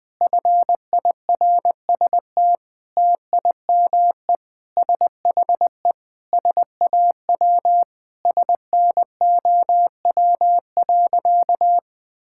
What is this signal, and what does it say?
Morse code, heard as a keyed tone, about 21 wpm, 700 Hz. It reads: FIRST TIME SHE SAW SNOW.
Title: Sentences with 5-Word Phrases - Spoken Beginning - 20wpm